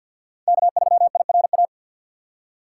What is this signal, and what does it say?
Morse code, heard as a keyed tone, about 51 wpm, 700 Hz. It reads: K3IRA
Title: USA Call Signs - 5 Characters - 2x Character Spacing - 50wpm